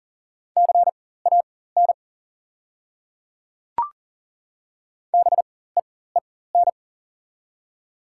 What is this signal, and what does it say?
Morse code, heard as a keyed tone, about 41 wpm, spaced as for 19 wpm, 690 Hz. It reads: CAN BEEN
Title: 4x Character Spacing - Top 100 Words - 40wpm